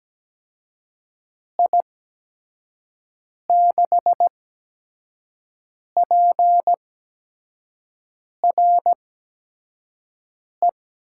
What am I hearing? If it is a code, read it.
Morse code, heard as a keyed tone, about 17 wpm, spaced as for 5 wpm, 700 Hz. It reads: I6PRE
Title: USA Call Signs - 6 Characters - 8x Character Spacing - 17wpm